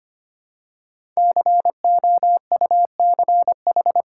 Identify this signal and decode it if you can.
Morse code, heard as a keyed tone, about 25 wpm, 690 Hz. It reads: COUCH
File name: practice / 25wpm / Easy Words with Hints - 25wpm